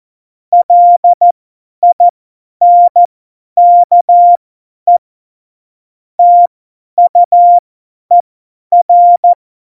Morse code, heard as a keyed tone, about 14 wpm, spaced as for 10 wpm, 700 Hz. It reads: LINKE TUER
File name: Morsekurs / Witze / Witz 14 Juli 2020, 50 F70